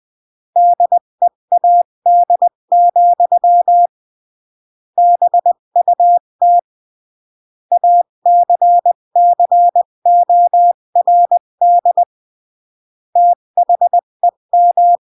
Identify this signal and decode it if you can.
Morse code, heard as a keyed tone, about 20 wpm, 700 Hz. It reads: DEAD, BUT ACCORD THEM
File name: CW / jodorowsky / 82 commandments 20-15